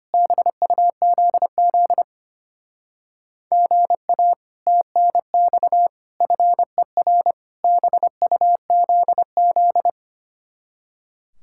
Morse code, heard as a keyed tone, about 31 wpm, 700 Hz. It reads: BUZZ GA TNX FER BUZZ
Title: Speed-Racing Common QSO Phrases - 25wpm